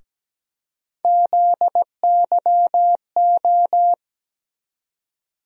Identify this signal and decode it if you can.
Morse code, heard as a keyed tone, about 17 wpm, 700 Hz. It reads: ZYO
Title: USA Call Sign Postfixes - 3 Letters 17wpm